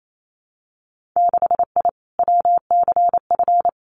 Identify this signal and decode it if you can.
Morse code, heard as a keyed tone, about 28 wpm, 700 Hz. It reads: 6I WCF
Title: Winter Field Day Exchanges Encore 28wpm